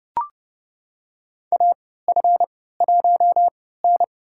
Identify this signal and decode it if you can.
Morse code, heard as a keyed tone, about 32 wpm, spaced as for 17 wpm, 700 Hz. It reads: AF1N
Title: USA Call Signs - 3x Character Spacing - 30wpm